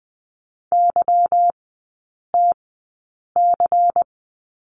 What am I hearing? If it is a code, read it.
Morse code, heard as a keyed tone, about 20 wpm, spaced as for 8 wpm, 700 Hz. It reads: YTC
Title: 166 Lesson 27e - Number 7 - Three Characters 20wpm